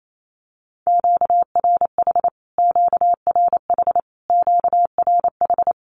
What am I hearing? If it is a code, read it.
Morse code, heard as a keyed tone, about 28 wpm, 700 Hz. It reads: QRH QRH QRH
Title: Q-Codes Repeated 3x 28wpm